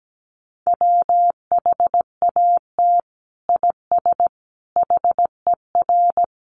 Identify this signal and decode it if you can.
Morse code, heard as a keyed tone, about 17 wpm, 700 Hz. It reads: WHAT IS HER
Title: Sentences from Top 1400 Words 17wpm